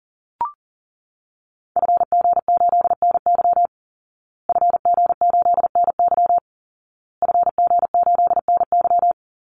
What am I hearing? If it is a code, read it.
Morse code, heard as a keyed tone, about 40 wpm, 700 Hz. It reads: FG8NY FG8NY FG8NY
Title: Call Signs Repeated 2x - 40wpm